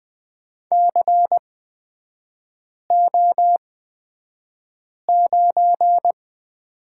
Morse code, heard as a keyed tone, about 20 wpm, spaced as for 5 wpm, 700 Hz. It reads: CO9
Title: 236 Lesson 38e - Letter q - Three Characters 20wpm